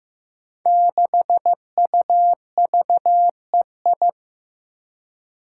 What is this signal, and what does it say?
Morse code, heard as a keyed tone, about 15 wpm, 700 Hz. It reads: 6UVEI
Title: Five Challenging Letter-Number Combinations 15wpm